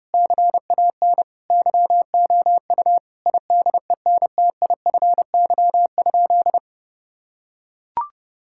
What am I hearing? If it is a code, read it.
Morse code, heard as a keyed tone, about 30 wpm, 700 Hz. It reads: CAN YOU IDENTIFY?
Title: Sentences from Top 1000 Words - Review Part 6 - 30wpm